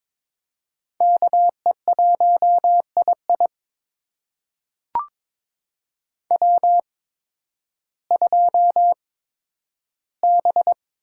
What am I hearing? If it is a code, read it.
Morse code, heard as a keyed tone, about 22 wpm, spaced as for 12 wpm, 700 Hz. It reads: KE1II W2B
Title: USA Call Signs - 5 Characters - 8x Character Spacing - 22wpm